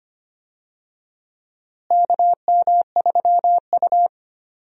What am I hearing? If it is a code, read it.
Morse code, heard as a keyed tone, about 25 wpm, 700 Hz. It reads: KM3U
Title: USA Call Signs - 4 Characters - Repeated 2x - 25wpm